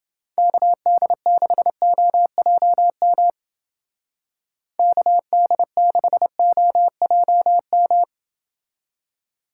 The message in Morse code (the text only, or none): KD6OJM KD6OJM